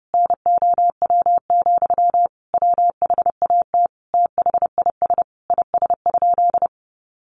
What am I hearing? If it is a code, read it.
Morse code, heard as a keyed tone, about 30 wpm, 700 Hz. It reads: NOW, WHAT THIS IS?
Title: Sentences from Top 1500 Words 30wpm